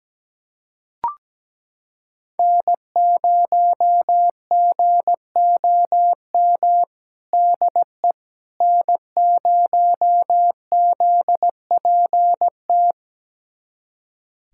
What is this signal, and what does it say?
Morse code, heard as a keyed tone, about 17 wpm, 700 Hz. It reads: N0GOM DE N0ZPT
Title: USA Callsign DE Callsign - 5 Characters - 17wpm